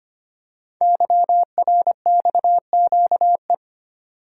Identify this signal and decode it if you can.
Morse code, heard as a keyed tone, about 25 wpm, 700 Hz. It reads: YRXQE
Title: Code Groups - 5 Letters - 25wpm